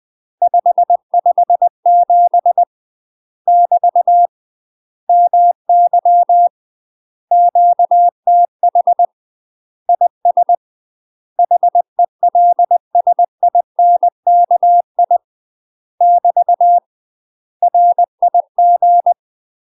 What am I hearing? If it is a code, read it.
Morse code, heard as a keyed tone, about 20 wpm, 700 Hz. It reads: 557 <BT> MY QTH IS HELSINKI <BT> RIG
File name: UK MASTER QSOs TRAINING / 20WPM / UK MASTER QSOs 0071